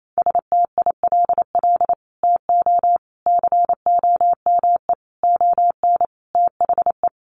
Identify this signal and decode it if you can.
Morse code, heard as a keyed tone, about 28 wpm, 700 Hz. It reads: STILL TO COME ON THE
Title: Sets of 5 Words Encore2 28wpm